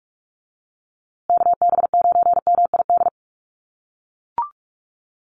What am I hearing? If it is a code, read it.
Morse code, heard as a keyed tone, about 44 wpm, 700 Hz. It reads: KB9MID
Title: USA Call Signs Repeated 2x 45wpm